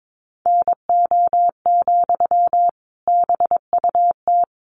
Morse code, heard as a keyed tone, about 22 wpm, 700 Hz. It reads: NO, BUT
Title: Sentences from Top 1100 Words 22wpm